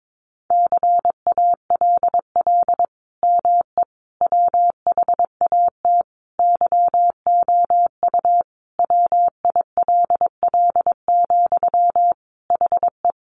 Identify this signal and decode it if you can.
Morse code, heard as a keyed tone, about 22 wpm, 700 Hz. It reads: CALL ME WHAT YOU WILL, HE
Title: Sentences from Top 200 Words 22wpm